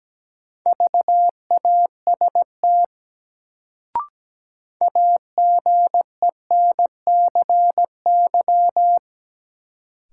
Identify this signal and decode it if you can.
Morse code, heard as a keyed tone, about 17 wpm, 700 Hz. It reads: VAST AGENCY
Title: Sentences from Top 1300 Words - New Word Drill Alt - 17wpm